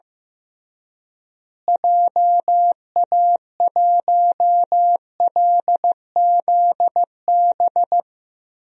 Morse code, heard as a keyed tone, about 15 wpm, 700 Hz. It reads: JA1LZB
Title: DXCC-100 Call Signs - 6 Characters - Repeated 2x - 15wpm